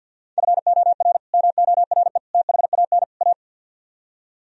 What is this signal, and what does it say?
Morse code, heard as a keyed tone, about 64 wpm, 700 Hz. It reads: FOR MORE THAN A